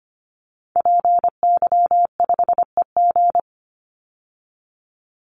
Morse code, heard as a keyed tone, about 25 wpm, 700 Hz. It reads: PY5EG
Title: Call Signs Repeated 2x - 25wpm